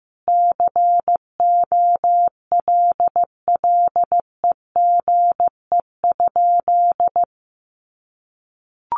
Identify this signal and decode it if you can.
Morse code, heard as a keyed tone, about 15 wpm, 700 Hz. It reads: COLLEGE?
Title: Sentences from Top 1300 Words 15wpm